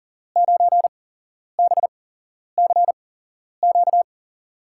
Morse code, heard as a keyed tone, about 40 wpm, spaced as for 11 wpm, 700 Hz. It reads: 9BCQ
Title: USA Call Signs - 8x Character Spacing - 40wpm